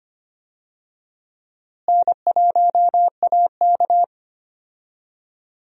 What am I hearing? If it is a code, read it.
Morse code, heard as a keyed tone, about 25 wpm, 700 Hz. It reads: N1AK
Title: USA Call Signs - 4 Characters - Repeated 3x - 25wpm